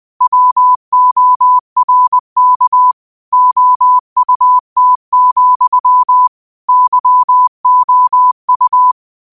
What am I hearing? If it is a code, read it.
Morse code, heard as a keyed tone, about 20 wpm, 1000 Hz. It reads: WORK OUT, YOU